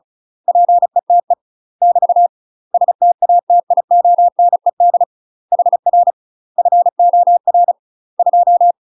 Morse code, heard as a keyed tone, about 36 wpm, 700 Hz. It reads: PETE <BT> STATIONED HR FOR 2
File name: UK MASTER QSOs TRAINING / 35WPM / UK MASTER QSOs 0029